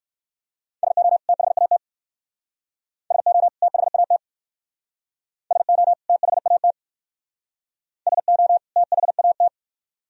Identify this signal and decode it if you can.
Morse code, heard as a keyed tone, about 56 wpm, 700 Hz. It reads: SO THAT SO THAT SO THAT SO THAT
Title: Speed-Racing - Sets of 2 Words 45wpm